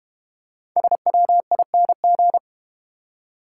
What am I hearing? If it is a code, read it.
Morse code, heard as a keyed tone, about 32 wpm, 700 Hz. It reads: SWING